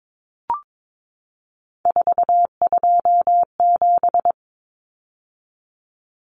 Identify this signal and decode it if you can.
Morse code, heard as a keyed tone, about 22 wpm, 700 Hz. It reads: 427
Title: Numbers - 3-Digit 22wpm